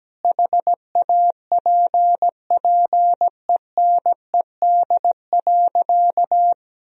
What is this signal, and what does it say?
Morse code, heard as a keyed tone, about 17 wpm, 700 Hz. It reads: HAPPENED.
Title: Sentences from Top 1000 Words - Review Part 4 - 17wpm